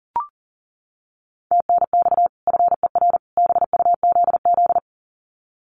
Morse code, heard as a keyed tone, about 40 wpm, 700 Hz. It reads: TNX FER BUZZ
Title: Common QSO Phrases 40wpm